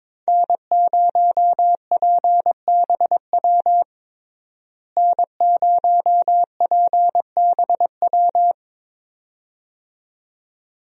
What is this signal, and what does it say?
Morse code, heard as a keyed tone, about 21 wpm, 700 Hz. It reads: N0PBW N0PBW
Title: USA Call Signs - 5 Characters - Repeated 2x - 22wpm